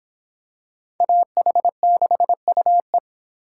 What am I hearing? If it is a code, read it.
Morse code, heard as a keyed tone, about 26 wpm, 700 Hz. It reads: AH6UE